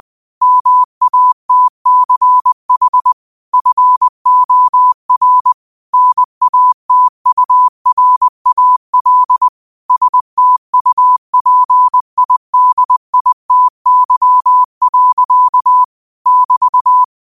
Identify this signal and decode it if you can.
Morse code, heard as a keyed tone, about 20 wpm, 1000 Hz. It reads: MATCH FOR NATURAL STUPIDITY. <BT>
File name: qotdfiles / 20wpm